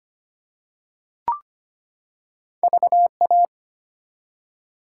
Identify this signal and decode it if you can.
Morse code, heard as a keyed tone, about 25 wpm, 700 Hz. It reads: VA
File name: practice / 25wpm / ARRL Sections v2 - 25wpm